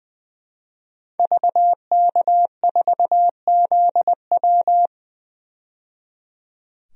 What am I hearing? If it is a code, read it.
Morse code, heard as a keyed tone, about 20 wpm, 700 Hz. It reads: VK4ZW
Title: DXCC-100 Call Signs - 5 Characters - 20wpm